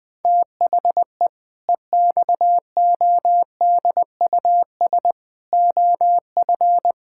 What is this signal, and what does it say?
Morse code, heard as a keyed tone, about 20 wpm, 700 Hz. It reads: THE EXODUS OF
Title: Pangram Challenge v2 - 20wpm